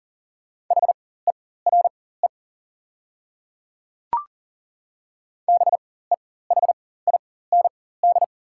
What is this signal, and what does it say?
Morse code, heard as a keyed tone, about 40 wpm, spaced as for 19 wpm, 700 Hz. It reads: HERE BEHIND
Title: 4x Character Spacing - Top 500 Words - 40wpm